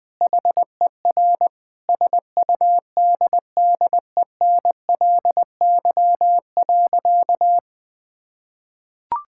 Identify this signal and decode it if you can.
Morse code, heard as a keyed tone, about 21 wpm, 700 Hz. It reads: HER SUDDENLY.
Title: Sentences with 2-Word Phrases - 20wpm